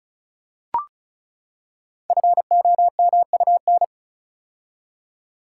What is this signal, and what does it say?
Morse code, heard as a keyed tone, about 36 wpm, 690 Hz. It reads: FOMUN
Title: Code Groups - 5 Letters - 35wpm